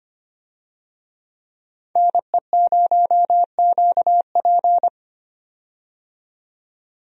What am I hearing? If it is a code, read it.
Morse code, heard as a keyed tone, about 25 wpm, 700 Hz. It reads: NE0QP
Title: USA Call Signs - 5 Characters - 25wpm